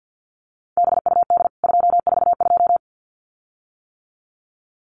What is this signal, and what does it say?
Morse code, heard as a keyed tone, about 48 wpm, 700 Hz. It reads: BUD 242
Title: CWT Exchange 50wpm